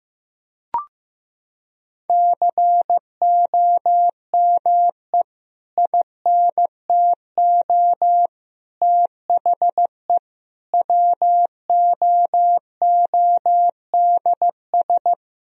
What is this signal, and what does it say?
Morse code, heard as a keyed tone, about 15 wpm, 700 Hz. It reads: COME INTO THE WOODS
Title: Sentences from Top 1000 Words - Review Part 2 - 15wpm